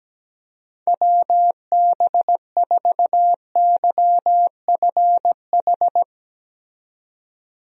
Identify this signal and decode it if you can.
Morse code, heard as a keyed tone, about 17 wpm, 700 Hz. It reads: WB4YFH
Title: USA Call Signs - 6 Characters - 17wpm